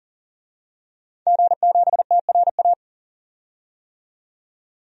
Morse code, heard as a keyed tone, about 40 wpm, 700 Hz. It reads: G7TRA
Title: DXCC-100 Call Signs - 5 Characters - 40wpm